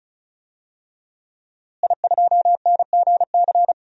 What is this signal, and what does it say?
Morse code, heard as a keyed tone, about 36 wpm, 700 Hz. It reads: I2NGC
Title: DXCC-100 Call Signs - 5 Characters - Repeated 3x - 35wpm